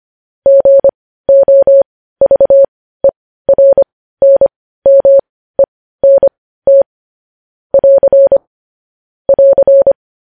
Morse code, heard as a keyed tone, about 25 wpm, spaced as for 15 wpm, 550 Hz. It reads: GOVERNMENT <AR> <AR>
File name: podcast / News headlines-1x-25x15